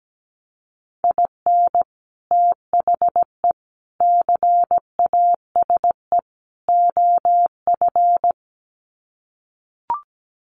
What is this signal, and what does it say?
Morse code, heard as a keyed tone, about 17 wpm, 700 Hz. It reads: IN THE CASE OF
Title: Sets of 4 Words 17wpm